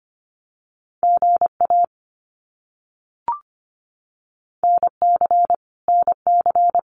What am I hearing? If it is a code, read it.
Morse code, heard as a keyed tone, about 25 wpm, 700 Hz. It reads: GA NC NC
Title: US States Repeated 3x 25wpm